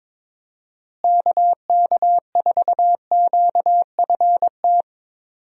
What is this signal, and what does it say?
Morse code, heard as a keyed tone, about 22 wpm, 700 Hz. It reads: KK4QFT
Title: USA Call Signs - 6 Characters - Repeated 3x - 22wpm